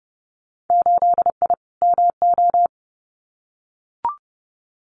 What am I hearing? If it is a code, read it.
Morse code, heard as a keyed tone, about 31 wpm, 700 Hz. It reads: 8I MO
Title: Winter Field Day Exchanges Encore 30wpm